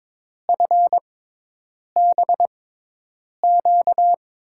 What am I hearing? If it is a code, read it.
Morse code, heard as a keyed tone, about 22 wpm, spaced as for 8 wpm, 700 Hz. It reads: FBQ
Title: USA Call Signs - 6x Character Spacing - 22wpm